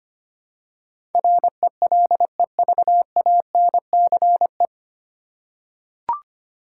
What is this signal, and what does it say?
Morse code, heard as a keyed tone, about 25 wpm, 700 Hz. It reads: RELEVANCE
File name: Nine Letter Words - 25wpm